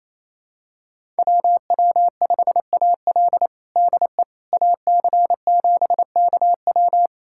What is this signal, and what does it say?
Morse code, heard as a keyed tone, about 28 wpm, 700 Hz. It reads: WW5AL DE AC7KW
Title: USA Callsign DE Callsign - 5 Characters - 28wpm